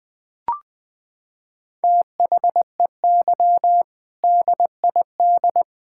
Morse code, heard as a keyed tone, about 20 wpm, 700 Hz. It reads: THEY DID
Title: Sentences from Top 1600 Words 20wpm